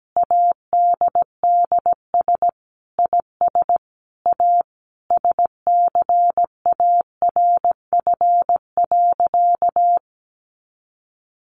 Morse code, heard as a keyed tone, about 17 wpm, 700 Hz. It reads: ADDS IS A SCARF.
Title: Easy Phrases 17wpm